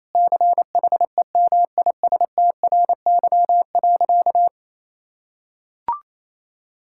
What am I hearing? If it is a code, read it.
Morse code, heard as a keyed tone, about 28 wpm, 690 Hz. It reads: CHEMISTRY.
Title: Sentences from Top 1000 Words - Review Part 6 - 28wpm